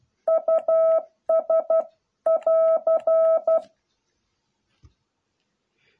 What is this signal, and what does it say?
Morse code, heard as a keyed tone, about 12 wpm, 660 Hz. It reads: US <AR>